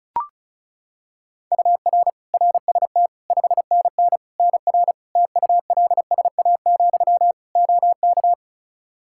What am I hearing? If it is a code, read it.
Morse code, heard as a keyed tone, about 36 wpm, 700 Hz. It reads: UR RST 5NN NR TULSA, OK
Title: Practice QSO - Exchange 1 - Beginning - 35wpm